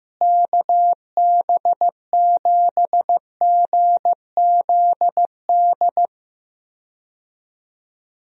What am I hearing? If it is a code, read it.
Morse code, heard as a keyed tone, about 15 wpm, 700 Hz. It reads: KB7GZD
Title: USA Call Signs - 6 Characters - Repeated 2x - 15wpm